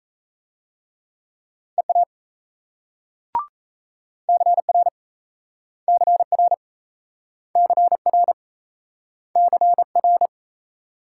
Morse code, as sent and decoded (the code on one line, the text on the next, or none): . .- / -.-. .-. / -.-. .-. / -.-. .-. / -.-. .-.
EA CR CR CR CR